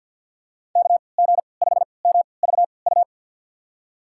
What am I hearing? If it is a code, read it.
Morse code, heard as a keyed tone, about 48 wpm, spaced as for 28 wpm, 690 Hz. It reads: KG5MVU
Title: USA Call Signs - 6 Characters - 3x Character Spacing - 50wpm